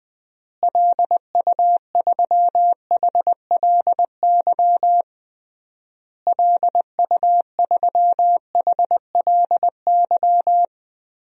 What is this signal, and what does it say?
Morse code, heard as a keyed tone, about 21 wpm, 700 Hz. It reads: LU3HLY LU3HLY
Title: DXCC-100 Call Signs - 6 Characters - Repeated 3x - 20wpm